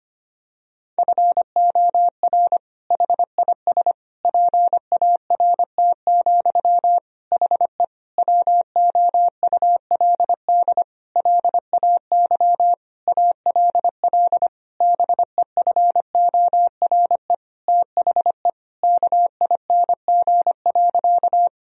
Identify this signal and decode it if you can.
Morse code, heard as a keyed tone, about 25 wpm, 700 Hz. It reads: FOR HIS PART, HE WOULD LAY ALL BEFORE THE KING.
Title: Sentences from Top 1000 Words - Review Part 2 - 25wpm